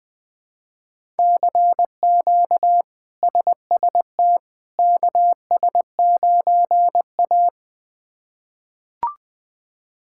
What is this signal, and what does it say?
Morse code, heard as a keyed tone, about 20 wpm, 700 Hz. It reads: CQ SST KS9A
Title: K1USN SST QSO Practice - 20wpm